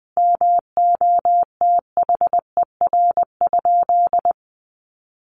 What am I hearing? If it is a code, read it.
Morse code, heard as a keyed tone, about 20 wpm, 700 Hz. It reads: MOTHER?
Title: Sentences from Top 1000 Words 20wpm